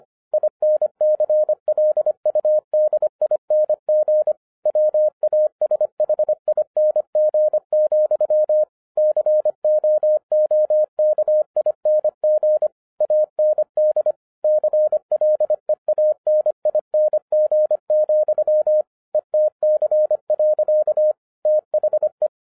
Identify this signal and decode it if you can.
Morse code, heard as a keyed tone, about 25 wpm, 610 Hz. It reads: INCLUDING WASHING, COOKING AND CLEANING, ETC. THE
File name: 2011-10-mp3 / n04-JUST MOVED 25wpm